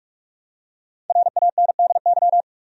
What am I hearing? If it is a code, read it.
Morse code, heard as a keyed tone, about 46 wpm, 700 Hz. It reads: RANDY
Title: USA Common Names - 45wpm